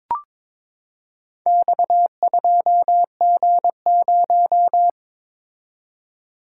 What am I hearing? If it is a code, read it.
Morse code, heard as a keyed tone, about 22 wpm, 700 Hz. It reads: X2G0
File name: practice / 22wpm / Four Letter-Number Combinations - 22wpm